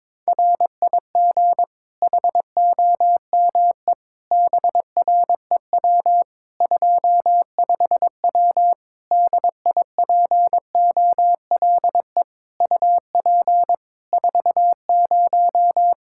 Morse code, heard as a keyed tone, about 22 wpm, 700 Hz. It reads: RIG HOME BREW 25W DIPOLE UP 40
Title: K1USN SST QSO Practice 22wpm